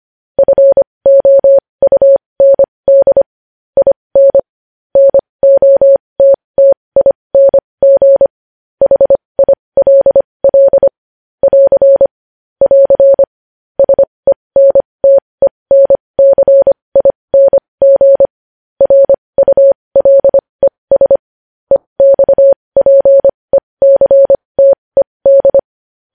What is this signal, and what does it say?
Morse code, heard as a keyed tone, about 25 wpm, spaced as for 20 wpm, 550 Hz. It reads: FOUND IN NOTTING HILL <AR> <AR> SENTENCING RULES EXPECTED
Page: News Headlines in Morse Code at 20 WPM